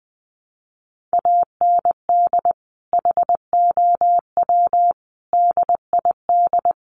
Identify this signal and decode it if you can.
Morse code, heard as a keyed tone, about 20 wpm, 700 Hz. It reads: AND HOW DID
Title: Sentences from Top 1200 Words 20wpm